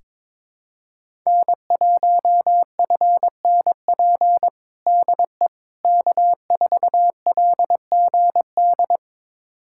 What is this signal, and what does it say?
Morse code, heard as a keyed tone, about 22 wpm, 700 Hz. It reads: N1FNP DE K4LGD
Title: USA Callsign DE Callsign - 5 Characters - 22wpm